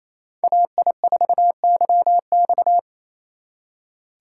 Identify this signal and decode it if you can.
Morse code, heard as a keyed tone, about 36 wpm, 700 Hz. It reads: AI4YX